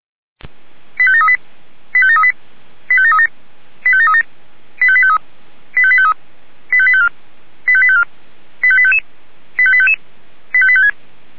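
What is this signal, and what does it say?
Five-tone selective call (ZVEI), heard as quick runs of five tones, tones of about 80 ms. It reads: 85738 85738 85737 85737 85853 85853 85854 85854 85855 85855 85856
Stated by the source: ZVEI five-tone call 85735 received with an SDR.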